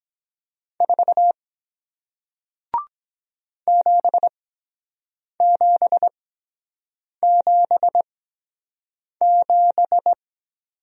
Morse code, heard as a keyed tone, about 23 wpm, spaced as for 7 wpm, 700 Hz. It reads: 4 7777